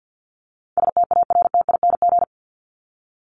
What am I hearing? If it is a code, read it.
Morse code, heard as a keyed tone, about 50 wpm, 700 Hz. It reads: STARTING
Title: ING Suffix - Top 100 Words with Base Words 50wpm